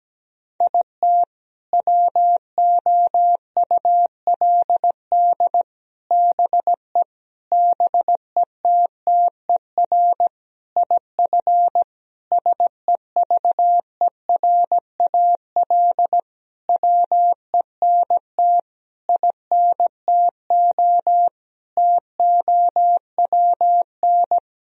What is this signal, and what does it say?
Morse code, heard as a keyed tone, about 17 wpm, 700 Hz. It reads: IT WOULD BE BETTER IF SEVERAL WENT INTO TOWN
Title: Sentences from Top 1000 Words - Review Part 2 - 17wpm